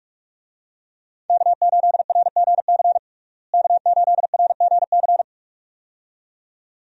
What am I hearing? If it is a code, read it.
Morse code, heard as a keyed tone, about 44 wpm, 700 Hz. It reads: K8RGC K8RGC
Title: USA Call Signs - 5 Characters - Repeated 2x - 45wpm